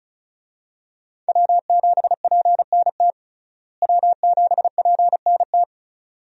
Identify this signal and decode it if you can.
Morse code, heard as a keyed tone, about 34 wpm, 700 Hz. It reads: W7PNT W7PNT
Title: USA Call Signs - 5 Characters - Repeated 2x - 35wpm